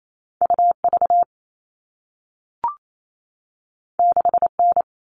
Morse code, heard as a keyed tone, about 28 wpm, 700 Hz. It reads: UV 6N
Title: All Cut Numbers - 2-Digit 28wpm